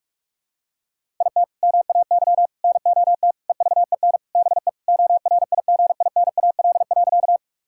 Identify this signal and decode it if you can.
Morse code, heard as a keyed tone, about 46 wpm, 700 Hz. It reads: IT MAY NOT EVEN BE ORIGINAL.
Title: Sentences from Top 1000 Words - Review Part 6 - 45wpm